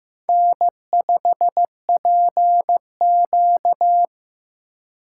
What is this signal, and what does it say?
Morse code, heard as a keyed tone, about 15 wpm, 700 Hz. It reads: N5PQ
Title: USA Call Signs - 2x Character Spacing - 15wpm